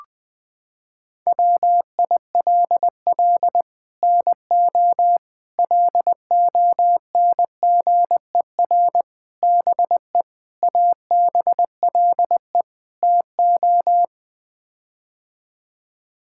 Morse code, heard as a keyed tone, about 21 wpm, 700 Hz. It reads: WILL NO LONGER BE ABLE TO
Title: Sets of 6 Words - 20wpm